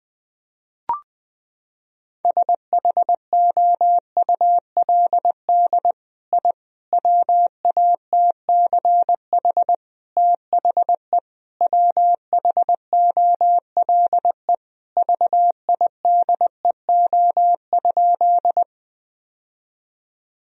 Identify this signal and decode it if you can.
Morse code, heard as a keyed tone, about 20 wpm, 700 Hz. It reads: SHOULD I WATCH THE WHOLE VIDEO?
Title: Sentences from Top 1000 Words - Review Part 4 - 20wpm